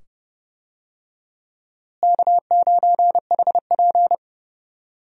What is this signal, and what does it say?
Morse code, heard as a keyed tone, about 30 wpm, 700 Hz. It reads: K9HP
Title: USA Call Signs - 4 Characters - Repeated 2x - 30wpm